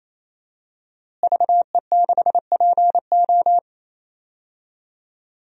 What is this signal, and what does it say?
Morse code, heard as a keyed tone, about 28 wpm, 700 Hz. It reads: VE6PO
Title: DXCC-100 Call Signs - 5 Characters - Repeated 3x - 28wpm